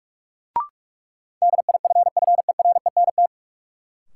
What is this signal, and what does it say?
Morse code, heard as a keyed tone, about 46 wpm, 700 Hz. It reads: DIFFERENT
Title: Sentences from Top 600 Words - New Word Drill - 45wpm